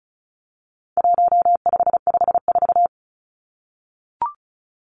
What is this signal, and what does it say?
Morse code, heard as a keyed tone, about 34 wpm, 700 Hz. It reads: 1554
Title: Numbers - 4-Digit 35wpm